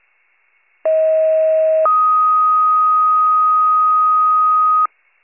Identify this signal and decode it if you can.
Two-tone page, heard as a one-second tone, then a three-second tone, about 640 then 1250 Hz. It is heard from 0.8 to 4.9 s.